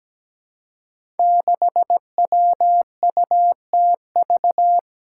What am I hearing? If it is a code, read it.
Morse code, heard as a keyed tone, about 17 wpm, 700 Hz. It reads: 6WUTV
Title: Five Challenging Letter-Number Combinations 17wpm